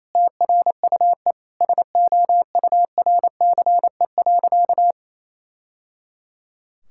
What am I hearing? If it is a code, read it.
Morse code, heard as a keyed tone, about 28 wpm, 700 Hz. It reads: TRUE SOURCE.
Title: Sentences from Top 1000 Words - Review Part 5 - 28wpm